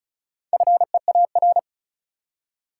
Morse code, heard as a keyed tone, about 36 wpm, 700 Hz. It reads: FEAR
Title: Four Letter Words - 35wpm